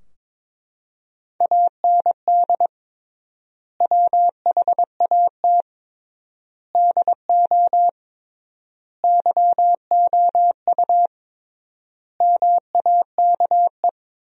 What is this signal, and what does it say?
Morse code, heard as a keyed tone, about 22 wpm, 700 Hz. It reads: AND WHAT DO YOU MAKE
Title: 3x Word-Spacing - Sentences from Top 100 Words - 22wpm